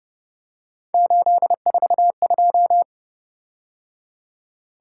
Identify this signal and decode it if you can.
Morse code, heard as a keyed tone, about 29 wpm, 700 Hz. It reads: 842
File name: Speed-Racing - Three Letter-Number Combinations - 25wpm